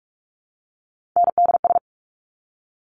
Morse code, heard as a keyed tone, about 46 wpm, 700 Hz. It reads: NDS
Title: Top Three Letter Combinations in Words 45wpm